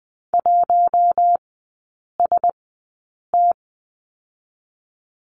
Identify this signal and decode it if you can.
Morse code, heard as a keyed tone, about 20 wpm, 700 Hz. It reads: 1 S T
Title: 047 Lesson 9e - Number 4 - Three Characters 20wpm